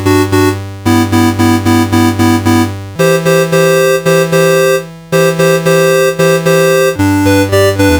There is no code.